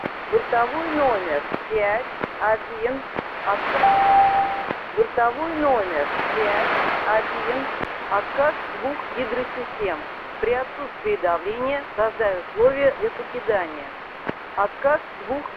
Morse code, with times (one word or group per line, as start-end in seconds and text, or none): none